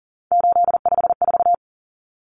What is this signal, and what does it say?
Morse code, heard as a keyed tone, about 40 wpm, 700 Hz. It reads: 854
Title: Common Cut Numbers - 3-Digit 40wpm